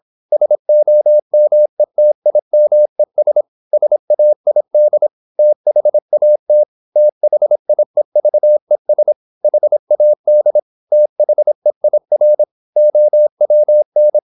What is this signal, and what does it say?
Morse code, heard as a keyed tone, about 26 wpm, 600 Hz. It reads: SOMETIMES SAID THAT THIEVES HAD THEIR OWN
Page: Never 26wpm | CW med Gnister